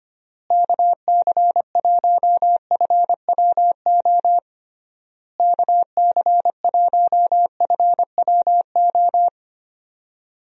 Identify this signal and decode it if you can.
Morse code, heard as a keyed tone, about 25 wpm, 700 Hz. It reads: KC1FWO KC1FWO
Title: USA Call Signs - 6 Characters - Repeated 3x - 25wpm